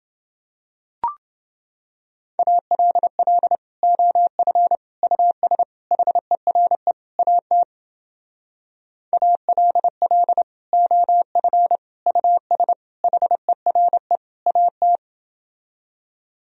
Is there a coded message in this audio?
Yes: Morse code from 2.4 to 15.0 s, heard as a keyed tone, about 31 wpm, 700 Hz. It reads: ALL OF US HERE AT ALL OF US HERE AT